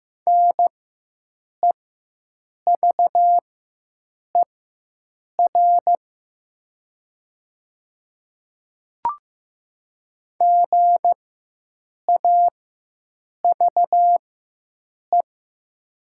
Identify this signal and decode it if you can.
Morse code, heard as a keyed tone, about 15 wpm, spaced as for 7 wpm, 700 Hz. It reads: NEVER GAVE
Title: 4x Character Spacing - Top 500 Words - 15wpm